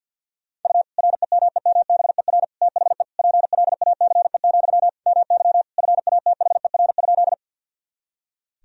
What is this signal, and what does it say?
Morse code, heard as a keyed tone, about 51 wpm, 700 Hz. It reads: U REMEMBER THE PLACE, MY FATHER?
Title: Sentences from Top 400 Words and Abbreviations 50wpm